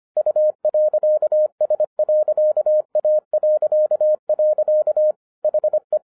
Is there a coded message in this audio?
Yes: Morse code from 0.2 to 6.0 s, heard as a keyed tone, about 25 wpm, 610 Hz. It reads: U.S.A.. HE